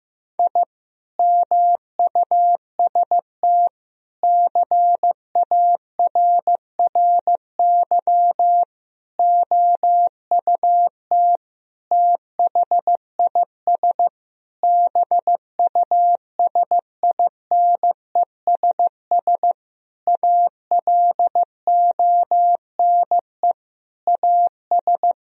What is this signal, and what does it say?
Morse code, heard as a keyed tone, about 15 wpm, 700 Hz. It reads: I MUST CARRY OUT THIS BUSINESS ALONE AS
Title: Sentences from Top 1000 Words - Review Part 2 - 15wpm